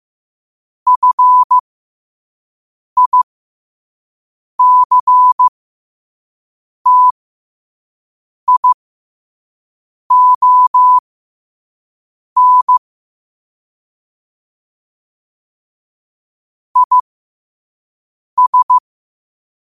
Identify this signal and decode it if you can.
Morse code, heard as a keyed tone, about 15 wpm, spaced as for 5 wpm, 1000 Hz. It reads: FICTION IS
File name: qotdfiles / 5wpm